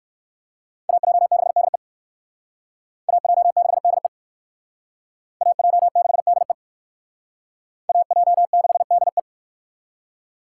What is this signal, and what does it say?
Morse code, heard as a keyed tone, about 60 wpm, 700 Hz. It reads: AJ6DE AJ6DE AJ6DE AJ6DE